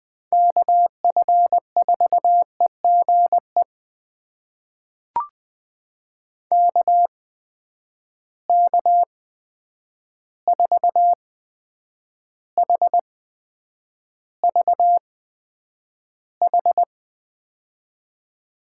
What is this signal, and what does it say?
Morse code, heard as a keyed tone, about 20 wpm, spaced as for 9 wpm, 700 Hz. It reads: KF4EGE KK4HVH
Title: USA Call Signs - 6 Characters - 8x Character Spacing - 20wpm